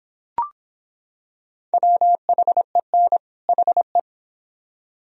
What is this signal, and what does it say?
Morse code, heard as a keyed tone, about 26 wpm, 700 Hz. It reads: WHEN HE